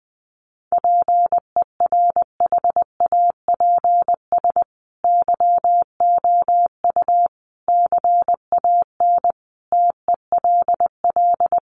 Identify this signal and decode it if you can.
Morse code, heard as a keyed tone, about 20 wpm, 700 Hz. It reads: PERHAPS YOU CAN TELL
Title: Sentences from Top 1200 Words 20wpm